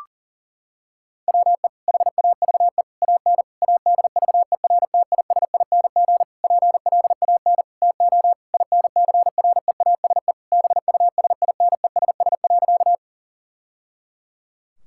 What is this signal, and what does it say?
Morse code, heard as a keyed tone, about 40 wpm, 700 Hz. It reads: WE HAVE AN ADVERTISING PLAN TO INCREASE BUSINESS.
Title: Sentences from Top 1000 Words - Review Part 6 - 40wpm